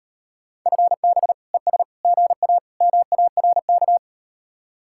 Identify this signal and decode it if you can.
Morse code, heard as a keyed tone, about 38 wpm, 700 Hz. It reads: FB ES GA MARK